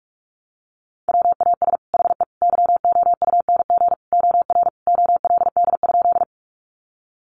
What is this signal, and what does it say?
Morse code, heard as a keyed tone, about 46 wpm, 700 Hz. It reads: WAS HE YOUNG OR OLD?
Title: Sentences from Top 400 Words 45wpm